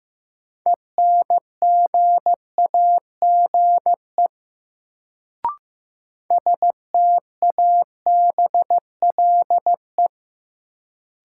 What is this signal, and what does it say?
Morse code, heard as a keyed tone, about 15 wpm, 700 Hz. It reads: ENGAGE STABLE
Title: Sentences from Top 1500 Words - New Word Drill Alt - 15wpm